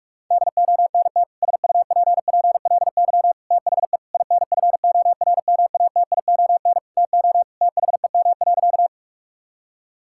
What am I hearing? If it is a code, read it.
Morse code, heard as a keyed tone, about 47 wpm, 700 Hz. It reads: DONT SUPPLY THE INFORMATION TO THEM.
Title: Sentences from Top 1000 Words - Review Part 5 - 45wpm